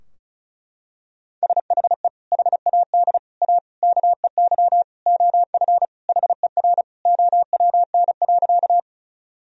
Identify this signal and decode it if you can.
Morse code, heard as a keyed tone, about 36 wpm, 700 Hz. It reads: SHE HAD A KEY OF HER OWN.
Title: Sentences from Top 1000 Words - Review Part 4 - 35wpm